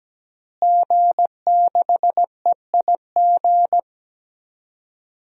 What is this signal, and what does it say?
Morse code, heard as a keyed tone, about 17 wpm, 700 Hz. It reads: G6EIG
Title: DXCC-100 Call Signs - 5 Characters - Repeated 3x - 17wpm